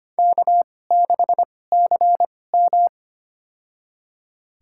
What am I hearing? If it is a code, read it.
Morse code, heard as a keyed tone, about 25 wpm, spaced as for 18 wpm, 700 Hz. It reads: K6CM
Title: USA Call Signs - 4 Characters - 2x Character Spacing - 25wpm